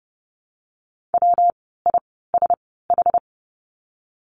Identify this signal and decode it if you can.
Morse code, heard as a keyed tone, about 30 wpm, spaced as for 17 wpm, 700 Hz. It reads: WISH
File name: course / 30:10wpm / 115 Lesson 19f - Letter w - Words 30wpm